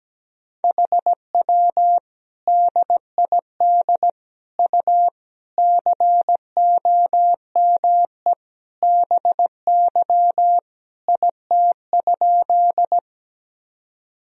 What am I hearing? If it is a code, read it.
Morse code, heard as a keyed tone, about 17 wpm, 700 Hz. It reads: HW DID U COME BY IT?
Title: Sentences from Top 100 Words and Abbreviations 17wpm